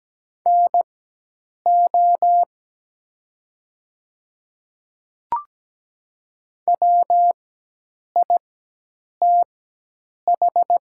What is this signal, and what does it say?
Morse code, heard as a keyed tone, about 17 wpm, spaced as for 8 wpm, 700 Hz. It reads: NO WITH
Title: 4x Character Spacing - Top 100 Words - 17wpm